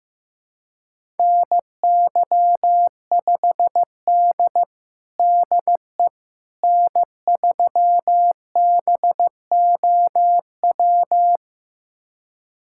Morse code, heard as a keyed tone, about 16 wpm, 700 Hz. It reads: NY5D DE N3BOW
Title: USA Callsign DE Callsign 15wpm